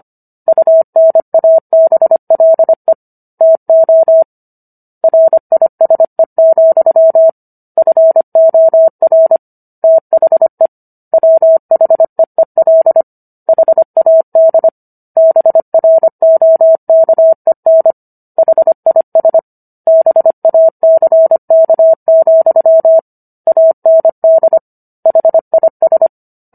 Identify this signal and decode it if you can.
Morse code, heard as a keyed tone, about 25 wpm, 650 Hz. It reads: UNABLE TO RISE, FOR THE WHEEL HAD BROKEN HIS BACK, AND HIS